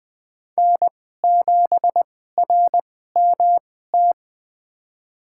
Morse code, heard as a keyed tone, about 20 wpm, spaced as for 14 wpm, 700 Hz. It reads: N7RMT
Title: USA Call Signs - 2x Character Spacing - 20wpm